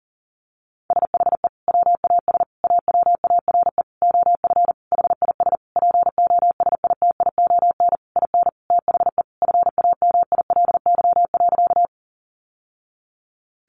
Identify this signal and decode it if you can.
Morse code, heard as a keyed tone, about 40 wpm, 700 Hz. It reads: SHE WAS AWARE OF HIS POSITION IN THE FAMILY.
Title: Sentences from Top 1200 Words 40wpm